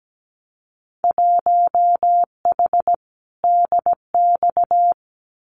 Morse code, heard as a keyed tone, about 17 wpm, 700 Hz. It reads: 1H DX
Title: Winter Field Day Exchanges 17wpm